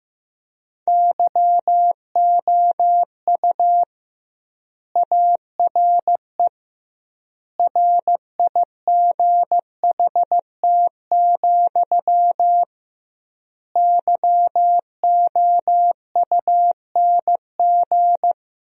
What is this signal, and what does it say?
Morse code, heard as a keyed tone, about 15 wpm, 700 Hz. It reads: YOU ARE RIGHT, YOUNG
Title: 2x Word-Spacing - Sentences from Top 400 Words - 15wpm